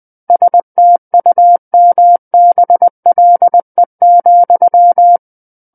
Morse code, heard as a keyed tone, about 20 wpm, 700 Hz. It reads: STUMBLE,
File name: 20 WPM